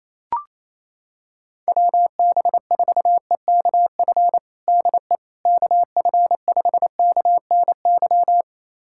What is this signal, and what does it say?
Morse code, heard as a keyed tone, about 28 wpm, 700 Hz. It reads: WB4EKF DE KF5KNY
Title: USA Callsign DE Callsign - 6 Characters - 28wpm